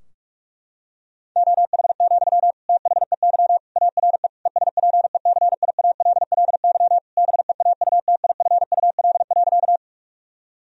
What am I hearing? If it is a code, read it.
Morse code, heard as a keyed tone, about 46 wpm, 700 Hz. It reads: OH, THEY ARE ESPECIALLY BEAUTIFUL.
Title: Sentences from Top 1000 Words - Review Part 4 - 45wpm